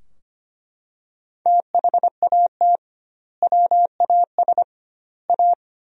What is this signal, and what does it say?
Morse code, heard as a keyed tone, about 25 wpm, 700 Hz. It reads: THAT WAS A
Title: 2x Word-Spacing - Sentences from Top 200 Words - 25wpm